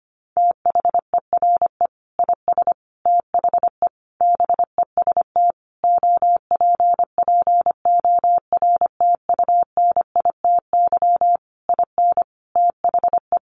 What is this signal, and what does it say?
Morse code, heard as a keyed tone, about 25 wpm, 700 Hz. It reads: THERE IS THE BEST OPPORTUNITY IN THE
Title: Sentences from Top 1500 Words 25wpm